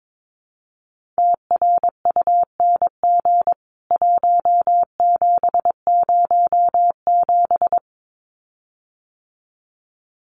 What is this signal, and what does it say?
Morse code, heard as a keyed tone, about 22 wpm, 700 Hz. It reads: TRUNG 1707
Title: CWT Exchange 22wpm